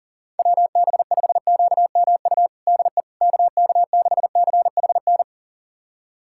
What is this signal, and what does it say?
Morse code, heard as a keyed tone, about 40 wpm, 700 Hz. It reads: WB5QMU DE KK6CHN
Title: USA Callsign DE Callsign 40wpm